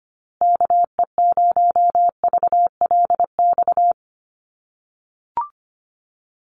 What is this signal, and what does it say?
Morse code, heard as a keyed tone, about 25 wpm, 700 Hz. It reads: KE0VLX
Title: USA Call Signs Repeated 3x 25wpm